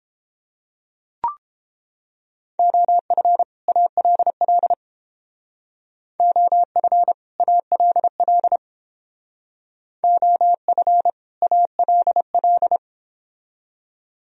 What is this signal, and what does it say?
Morse code, heard as a keyed tone, about 31 wpm, 700 Hz. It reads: OF ALL OF ALL OF ALL